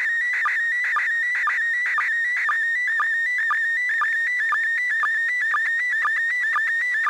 SSTV (slow-scan television), PD 120.